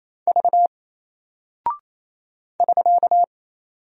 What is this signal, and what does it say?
Morse code, heard as a keyed tone, about 27 wpm, 700 Hz. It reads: V <SK>
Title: Single Character (Letter:Number:Punctuation:Prosign) Alt 28wpm